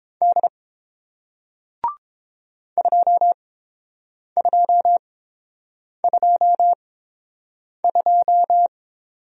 Morse code, heard as a keyed tone, about 30 wpm, spaced as for 7 wpm, 700 Hz. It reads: D 2222